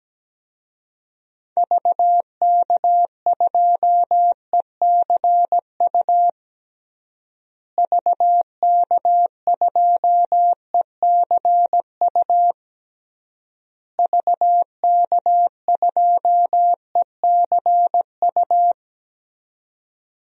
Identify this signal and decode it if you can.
Morse code, heard as a keyed tone, about 17 wpm, 700 Hz. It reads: VK2ECU VK2ECU VK2ECU
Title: DXCC-100 Call Signs - 6 Characters - Repeated 3x - 17wpm